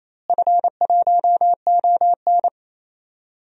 Morse code, heard as a keyed tone, about 28 wpm, 700 Hz. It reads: F1ON
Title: DXCC-100 Call Signs - 4 Characters - Repeated 3x - 28wpm